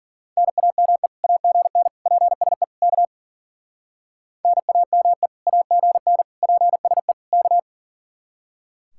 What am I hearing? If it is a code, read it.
Morse code, heard as a keyed tone, about 50 wpm, 700 Hz. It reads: NAME AGN PSE K NAME AGN PSE K